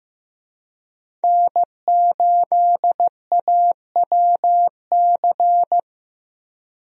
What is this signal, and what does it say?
Morse code, heard as a keyed tone, about 15 wpm, 700 Hz. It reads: N8AWC
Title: USA Call Signs - 5 Characters - Repeated 2x - 15wpm